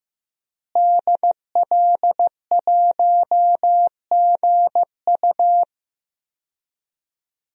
Morse code, heard as a keyed tone, about 15 wpm, 700 Hz. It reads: DL1GU
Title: DXCC-100 Call Signs - 5 Characters - Repeated 2x - 15wpm